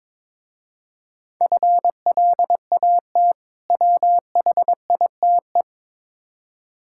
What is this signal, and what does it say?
Morse code, heard as a keyed tone, about 22 wpm, 700 Hz. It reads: FLAT WHITE
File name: Difficult Words with Hints - 22wpm